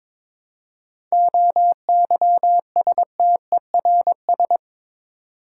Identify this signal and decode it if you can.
Morse code, heard as a keyed tone, about 22 wpm, 690 Hz. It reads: OYSTERS
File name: Difficult Words with Hints - 22wpm